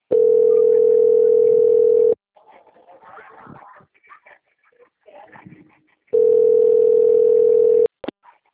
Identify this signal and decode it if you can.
Ringback tone (North America), 440 + 480 Hz.